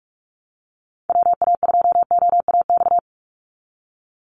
Morse code, heard as a keyed tone, about 44 wpm, 700 Hz. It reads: WA2OAX
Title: Call Signs Repeated 2x - 45wpm